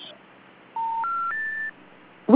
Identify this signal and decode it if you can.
Special information tone.